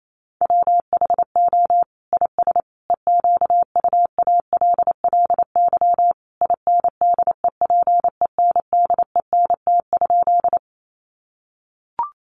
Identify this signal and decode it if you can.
Morse code, heard as a keyed tone, about 28 wpm, 700 Hz. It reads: WHO IS EQUALLY INDEPENDENT?
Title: Sentences from Top 1400 Words 28wpm